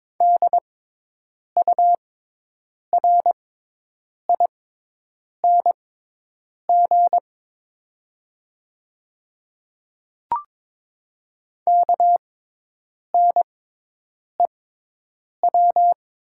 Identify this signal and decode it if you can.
Morse code, heard as a keyed tone, about 22 wpm, spaced as for 8 wpm, 700 Hz. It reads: DURING KNEW
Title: 6x Character Spacing - Top 500 Words - 22wpm